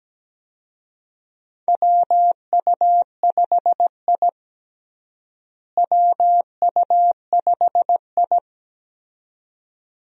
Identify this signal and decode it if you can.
Morse code, heard as a keyed tone, about 17 wpm, 700 Hz. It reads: WU5I WU5I
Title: USA Call Signs - 4 Characters - Repeated 2x - 17wpm